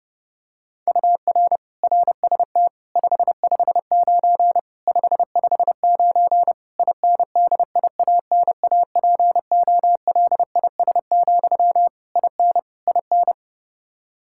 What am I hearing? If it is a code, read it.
Morse code, heard as a keyed tone, about 30 wpm, 700 Hz. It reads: UR RST 559 559 INDIANAPOLIS, IN IN
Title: Practice QSO - Exchange 1 - Beginning - 30wpm